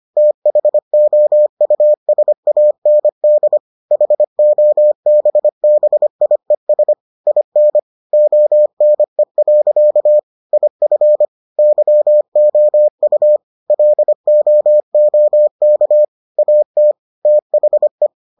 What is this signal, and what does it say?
Morse code, heard as a keyed tone, about 25 wpm, 600 Hz. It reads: THOUSAND HOBBIES IN ONE. IF YOU LOOK AT THE